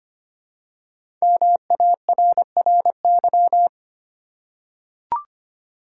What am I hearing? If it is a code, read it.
Morse code, heard as a keyed tone, about 25 wpm, 700 Hz. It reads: MARRY
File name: 5 Letter Words - First 3 Letters Given - 25wpm